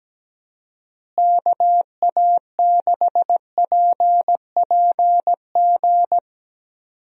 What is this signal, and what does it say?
Morse code, heard as a keyed tone, about 17 wpm, 700 Hz. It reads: KA6PPG
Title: USA Call Signs - 6 Characters - Repeated 2x - 17wpm